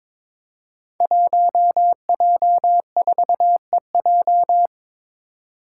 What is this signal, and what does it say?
Morse code, heard as a keyed tone, about 22 wpm, 700 Hz. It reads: 1J4EJ
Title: Five Challenging Letter-Number Combinations 22wpm